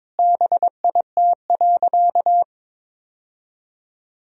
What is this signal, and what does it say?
Morse code, heard as a keyed tone, about 22 wpm, 700 Hz. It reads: BIT.
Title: Sentences from Top 1000 Words - Review Part 1 - 22wpm